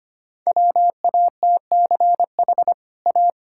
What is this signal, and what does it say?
Morse code, heard as a keyed tone, about 25 wpm, 700 Hz. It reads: WATCH A
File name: Easy Sentences with Hints - 25wpm